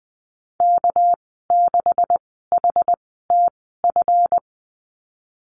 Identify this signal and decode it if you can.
Morse code, heard as a keyed tone, about 20 wpm, spaced as for 15 wpm, 700 Hz. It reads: K6HTF
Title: 207 Lesson 33g - Letter j - Call Signs 20wpm